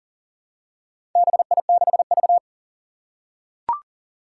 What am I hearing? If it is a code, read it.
Morse code, heard as a keyed tone, about 40 wpm, 700 Hz. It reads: BI6V
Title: Four Challenging Letter-Number Combinations 40wpm